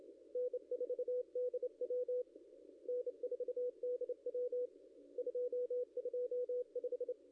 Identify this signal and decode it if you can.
Morse code, heard as a keyed tone, about 30 wpm, 500 Hz, with radio noise. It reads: N4DW N4DW 225
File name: contests / ss-cw-k5tr-op-n5rz -0700